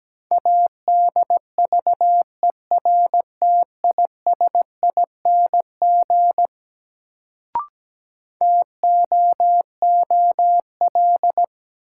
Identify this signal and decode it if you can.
Morse code, heard as a keyed tone, about 17 wpm, 700 Hz. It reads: ADVERTISING TOOL
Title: Sentences from Top 1000 Words - Review Part 6 - New Word Drill - 17wpm